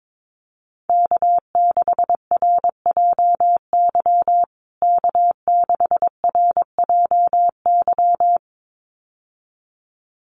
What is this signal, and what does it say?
Morse code, heard as a keyed tone, about 22 wpm, 700 Hz. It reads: K6RJY K6RJY
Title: USA Call Signs Repeated 2x 22wpm